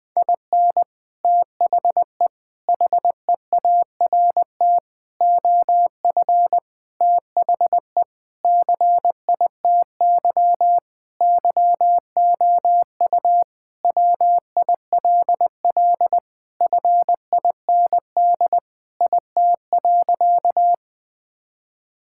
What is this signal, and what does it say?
Morse code, heard as a keyed tone, about 20 wpm, 700 Hz. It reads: IN THE HEART OF THE CITY YOU WILL FIND IT.
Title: Sentences with 3-Word Phrases - 20wpm